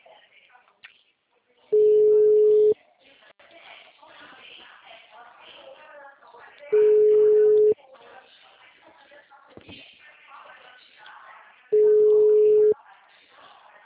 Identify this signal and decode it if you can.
Ringback tone (Europe, 425 Hz), with speech.